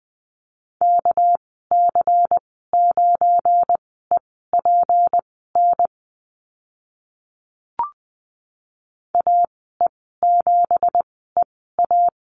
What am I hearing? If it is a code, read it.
Morse code, heard as a keyed tone, about 20 wpm, spaced as for 14 wpm, 700 Hz. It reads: KC9EPN AE7EA
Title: 213 Lesson 34g - Number 8 - Call Signs 20wpm